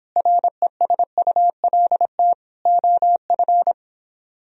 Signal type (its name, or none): Morse code